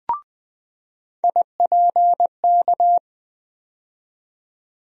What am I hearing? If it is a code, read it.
Morse code, heard as a keyed tone, about 20 wpm, 700 Hz. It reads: IPK
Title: USA Call Sign Postfixes - 3 Letters 20wpm